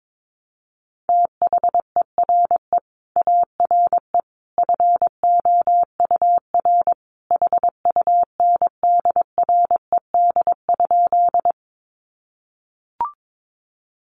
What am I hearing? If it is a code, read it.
Morse code, heard as a keyed tone, about 22 wpm, 700 Hz. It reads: THERE ARE FOUR HUNDRED?
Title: Sentences from Top 400 Words 22wpm